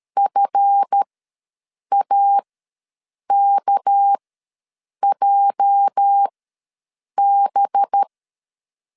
Morse code, heard as a keyed tone, about 13 wpm, spaced as for 7 wpm, 800 Hz. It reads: FAKJB
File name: LEZIONI / 8 YJ45 / 8 YJ ws 12wpm 3342